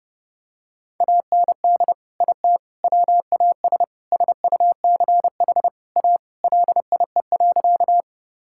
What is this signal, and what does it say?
Morse code, heard as a keyed tone, about 32 wpm, 700 Hz. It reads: AND IT WAS SUCH A LIE.
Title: Sentences from Top 1600 Words 30wpm